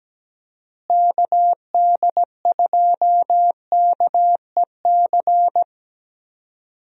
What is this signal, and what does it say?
Morse code, heard as a keyed tone, about 17 wpm, 700 Hz. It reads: KD2KEC
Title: USA Call Signs - 6 Characters - Repeated 2x - 17wpm